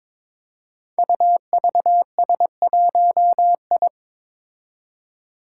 Morse code, heard as a keyed tone, about 22 wpm, 700 Hz. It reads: UVS1I
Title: Five Challenging Letter-Number Combinations 22wpm